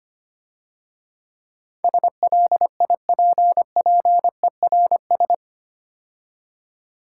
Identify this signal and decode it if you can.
Morse code, heard as a keyed tone, about 27 wpm, 700 Hz. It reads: SLIPPERS
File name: Easy Words - Speed-Racing with Hints - 25wpm